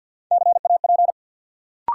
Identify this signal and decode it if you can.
Morse code, heard as a keyed tone, about 52 wpm, 700 Hz. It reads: CAP
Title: Sentences from Top 1600 Words - New Word Drill Alt - 50wpm